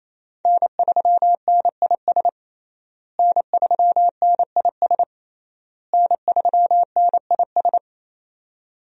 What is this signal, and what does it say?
Morse code, heard as a keyed tone, about 28 wpm, 700 Hz. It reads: N3NIS N3NIS N3NIS
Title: USA Call Signs - 5 Characters - Repeated 3x - 28wpm